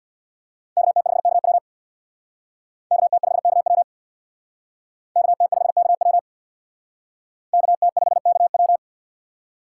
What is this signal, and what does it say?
Morse code, heard as a keyed tone, about 65 wpm, 700 Hz. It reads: KT5KW KT5KW KT5KW KT5KW